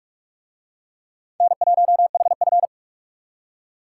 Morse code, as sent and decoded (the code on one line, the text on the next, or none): -. .---- .... ..-.
N1HF